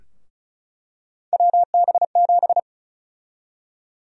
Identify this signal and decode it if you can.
Morse code, heard as a keyed tone, about 36 wpm, 700 Hz. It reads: WB7
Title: USA Call Sign Prefixes - Repeated 2x - 35wpm